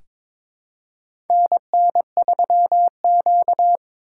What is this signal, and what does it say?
Morse code, heard as a keyed tone, about 22 wpm, 700 Hz. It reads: NN3Q
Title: USA Call Signs - 4 Characters - 8x Character Spacing - 22wpm